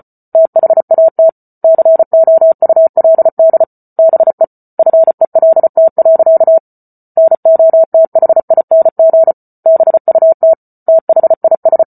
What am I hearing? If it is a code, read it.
Morse code, heard as a keyed tone, about 34 wpm, 650 Hz. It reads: THAT COULD BE FELT. NOTHING BUT THIS